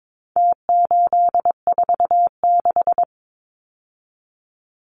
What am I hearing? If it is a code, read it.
Morse code, heard as a keyed tone, about 22 wpm, 700 Hz. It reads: T846
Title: Numbers - Contest Style - 4-Digit 22wpm